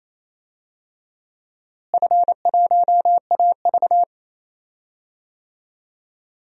Morse code, heard as a keyed tone, about 27 wpm, 700 Hz. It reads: F1AV
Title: DXCC-100 Call Signs - 4 Characters - Repeated 3x - 28wpm